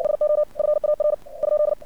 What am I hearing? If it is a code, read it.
Morse code, heard as a keyed tone, about 56 wpm, 600 Hz. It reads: SO FAR ?
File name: archived / CRSnet